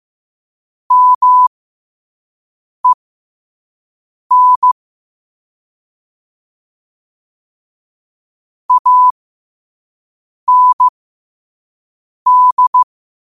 Morse code, heard as a keyed tone, about 15 wpm, spaced as for 5 wpm, 1000 Hz. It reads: MEN AND
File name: qotdfiles / 5wpm